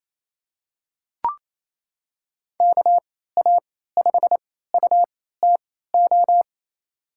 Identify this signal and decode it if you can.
Morse code, heard as a keyed tone, about 28 wpm, spaced as for 16 wpm, 700 Hz. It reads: KA5UTO
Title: USA Call Signs - 3x Character Spacing - 28wpm